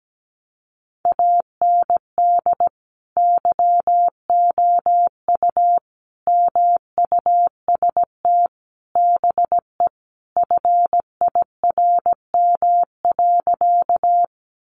Morse code, heard as a keyed tone, about 17 wpm, 700 Hz. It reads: AND YOU MUST BE FIRM.
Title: Sentences from Top 1400 Words 17wpm